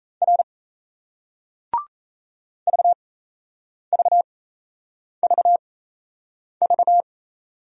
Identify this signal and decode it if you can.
Morse code, heard as a keyed tone, about 38 wpm, spaced as for 8 wpm, 700 Hz. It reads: R VVVV